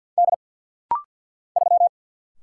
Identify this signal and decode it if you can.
Morse code, heard as a keyed tone, about 50 wpm, 700 Hz. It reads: D 3
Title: Single Letter-Number Alt - Rapid-Fire 50wpm